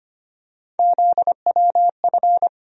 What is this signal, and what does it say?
Morse code, heard as a keyed tone, about 25 wpm, 700 Hz. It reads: ZWF
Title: USA Call Sign Postfixes - 3 Letters 25wpm